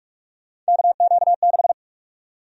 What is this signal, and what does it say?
Morse code, heard as a keyed tone, about 46 wpm, 700 Hz. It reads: KQ6
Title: USA Call Sign Prefixes - 45wpm